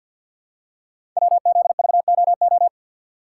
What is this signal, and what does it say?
Morse code, heard as a keyed tone, about 53 wpm, 700 Hz. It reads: WZVOO
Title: Code Groups - 5 Letter:Numbers - 50wpm